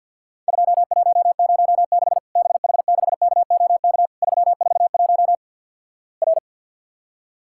Morse code, heard as a keyed tone, about 52 wpm, 700 Hz. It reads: 2106 BH6KOK 341 R
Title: Worked All Europe DX Contest - QTC Exchange - 50wpm